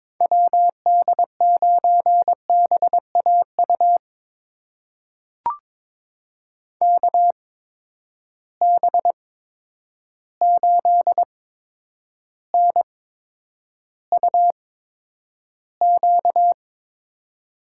Morse code, heard as a keyed tone, about 22 wpm, spaced as for 9 wpm, 700 Hz. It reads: WD9BAU KB8NUQ
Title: USA Call Signs - 6 Characters - 8x Character Spacing - 22wpm